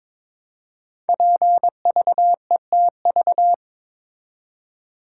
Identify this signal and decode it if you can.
Morse code, heard as a keyed tone, about 23 wpm, 700 Hz. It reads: PVETV
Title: Code Groups - 5 Letters - 22wpm